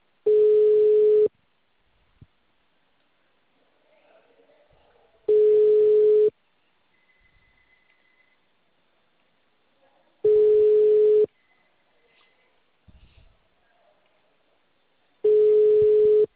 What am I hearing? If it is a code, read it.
Ringback tone (Europe, 425 Hz).